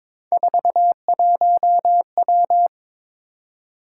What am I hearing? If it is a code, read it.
Morse code, heard as a keyed tone, about 22 wpm, 700 Hz. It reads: 41W
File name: practice / 22wpm / Three Letter-Number Combinations - 22wpm